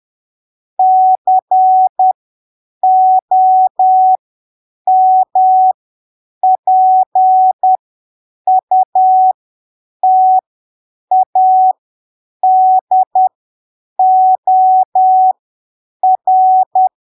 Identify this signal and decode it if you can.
Morse code, heard as a keyed tone, about 10 wpm, spaced as for 7 wpm, 750 Hz. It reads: COMPUTADOR
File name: arquivos / texto / texto 09 10ppm